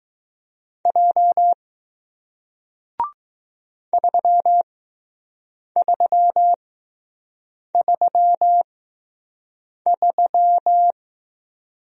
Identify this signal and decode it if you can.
Morse code, heard as a keyed tone, about 20 wpm, spaced as for 6 wpm, 700 Hz. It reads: J 3333